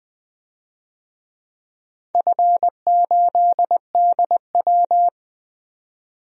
Morse code, heard as a keyed tone, about 20 wpm, 700 Hz. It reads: F8DW
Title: DXCC-100 Call Signs - 4 Characters - 20wpm